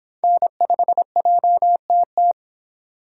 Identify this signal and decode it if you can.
Morse code, heard as a keyed tone, about 24 wpm, 700 Hz. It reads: N5JTT